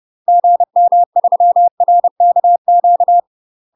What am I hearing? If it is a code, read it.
Morse code, heard as a keyed tone, about 30 wpm, 700 Hz. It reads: GM3RKQ
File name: UK MASTER QSOs TRAINING / 30WPM / UK MASTER QSOs 0000